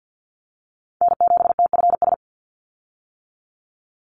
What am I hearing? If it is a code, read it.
Morse code, heard as a keyed tone, about 48 wpm, 700 Hz. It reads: N7TFS
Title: USA Call Signs 50wpm